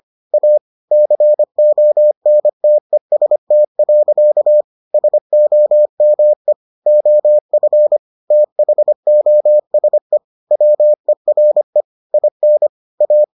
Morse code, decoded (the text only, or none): A CONTEST. SOME OF THOSE WERE IN A